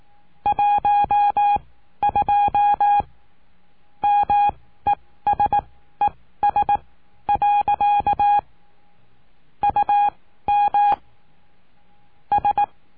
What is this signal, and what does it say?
Morse code, heard as a keyed tone, about 18 wpm, spaced as for 13 wpm, 820 Hz. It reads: 12 MESES. UM S
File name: cursos / CURSO DE TELEGRAFIA / Telegrafia - Trilha 20